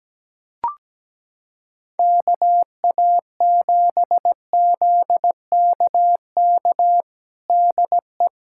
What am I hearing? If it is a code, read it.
Morse code, heard as a keyed tone, about 17 wpm, 700 Hz. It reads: KA7ZKK DE
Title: USA Callsign DE Callsign - 6 Characters - 17wpm